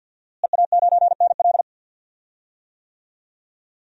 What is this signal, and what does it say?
Morse code, heard as a keyed tone, about 52 wpm, 700 Hz. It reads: EA9NL
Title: DXCC-100 Call Signs - 5 Characters - Repeated 3x - 50wpm